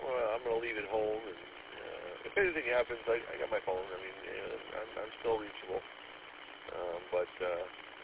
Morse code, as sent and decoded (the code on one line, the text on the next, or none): none